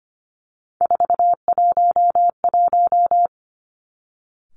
Morse code, heard as a keyed tone, about 25 wpm, 700 Hz. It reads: 411
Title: Three Challenging Letter-Number Combinations 25wpm